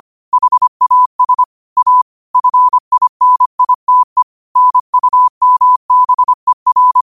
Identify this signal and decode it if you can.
Morse code, heard as a keyed tone, about 25 wpm, 1000 Hz. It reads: HAS A FINITE NUMBER
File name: qotdfiles / 25wpm